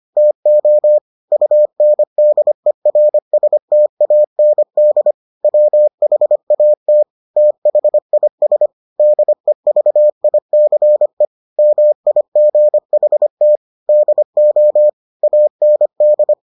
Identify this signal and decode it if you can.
Morse code, heard as a keyed tone, about 25 wpm, 600 Hz. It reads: TO UNDERSTAND WHAT THIS DEVICE MIGHT DO AND